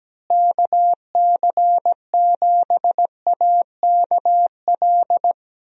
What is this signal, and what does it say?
Morse code, heard as a keyed tone, about 17 wpm, 700 Hz. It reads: KC7AKL
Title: USA Call Signs - 6 Characters - Repeated 2x - 17wpm